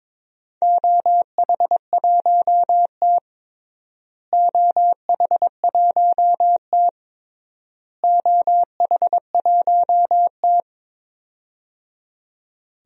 Morse code, heard as a keyed tone, about 22 wpm, 700 Hz. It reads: OH1T OH1T OH1T
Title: DXCC-100 Call Signs - 4 Characters - Repeated 3x - 22wpm